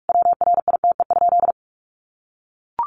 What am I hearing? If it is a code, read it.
Morse code, heard as a keyed tone, about 44 wpm, 700 Hz. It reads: WRITE?
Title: Sentences from Top 600 Words 45wpm